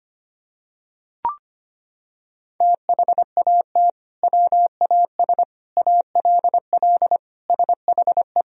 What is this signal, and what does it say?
Morse code, heard as a keyed tone, about 25 wpm, 700 Hz. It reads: THAT WAS ALL SHE
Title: Sentences from Top 100 Words and Abbreviations 25wpm